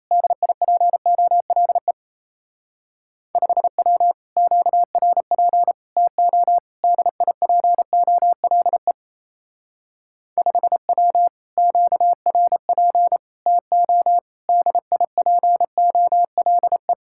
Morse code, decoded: DIPOLE 5W QRP TO DIPOLE 5W QRP TO DIPOLE

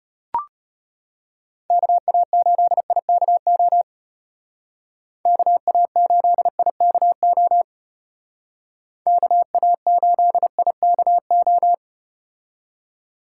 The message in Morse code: KA8IKO KA8IKO KA8IKO

